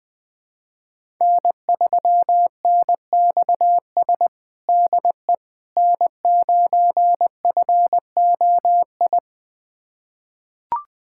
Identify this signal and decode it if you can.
Morse code, heard as a keyed tone, about 21 wpm, 700 Hz. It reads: N3NXS DE N9FOI
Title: USA Callsign DE Callsign - 5 Characters - 20wpm